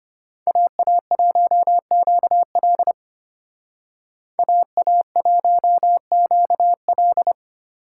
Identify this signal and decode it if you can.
Morse code, heard as a keyed tone, about 32 wpm, 700 Hz. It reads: AA1QL AA1QL